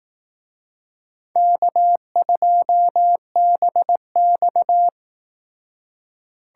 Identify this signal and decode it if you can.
Morse code, heard as a keyed tone, about 18 wpm, 700 Hz. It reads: K2BX
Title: Speed Racing - USA 1x2 Call Signs 15wpm